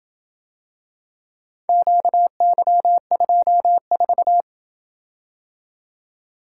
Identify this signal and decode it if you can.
Morse code, heard as a keyed tone, about 27 wpm, 700 Hz. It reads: QY24